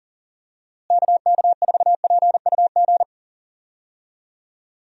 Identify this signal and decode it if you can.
Morse code, heard as a keyed tone, about 40 wpm, 700 Hz. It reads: KK4PUG
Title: USA Call Signs - 6 Characters - Repeated 2x - 40wpm